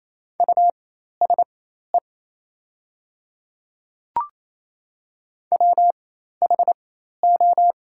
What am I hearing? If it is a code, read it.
Morse code, heard as a keyed tone, about 28 wpm, spaced as for 13 wpm, 700 Hz. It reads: USE WHO
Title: 4x Character Spacing - Top 100 Words - 28wpm